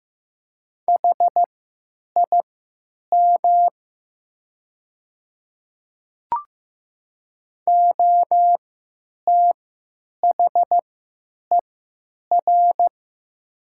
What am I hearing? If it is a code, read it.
Morse code, heard as a keyed tone, about 15 wpm, spaced as for 9 wpm, 700 Hz. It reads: HIM OTHER
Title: 3x Character Spacing - Top 100 Words - 15wpm